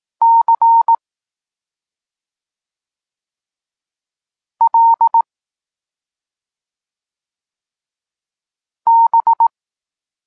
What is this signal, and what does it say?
Morse code, heard as a keyed tone, about 18 wpm, 920 Hz. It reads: C L B
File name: Tekst-34 4